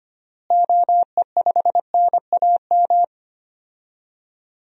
Morse code, heard as a keyed tone, about 25 wpm, 700 Hz. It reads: OE5NAM
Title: DXCC-100 Call Signs - 6 Characters - Repeated 2x - 25wpm